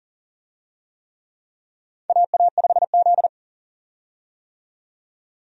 Morse code, heard as a keyed tone, about 40 wpm, 700 Hz. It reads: AA5Z
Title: USA Call Signs - 4 Characters - 40wpm